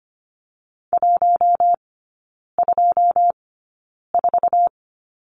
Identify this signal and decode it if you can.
Morse code, heard as a keyed tone, about 25 wpm, spaced as for 9 wpm, 700 Hz. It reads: 124
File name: course / 25:10wpm / 093 Lesson 16e - Letter c - Three Characters 25wpm